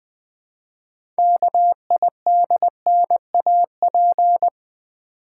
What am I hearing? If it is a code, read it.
Morse code, heard as a keyed tone, about 20 wpm, 700 Hz. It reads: KIDNAP
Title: 6 Letter Words - First 3 Letters Given - 20wpm